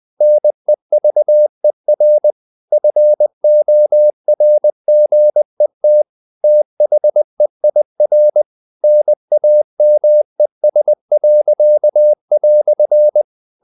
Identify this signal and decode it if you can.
Morse code, heard as a keyed tone, about 20 wpm, 600 Hz. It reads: NEVER FORGET THEIR NAMES."
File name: CW / Practice / EnglishQuotations0320